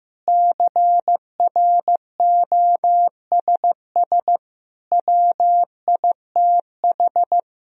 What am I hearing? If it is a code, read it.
Morse code, heard as a keyed tone, about 15 wpm, 700 Hz. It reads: CROSS WITH
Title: Sentences from Top 1000 Words - Review Part 1 - 15wpm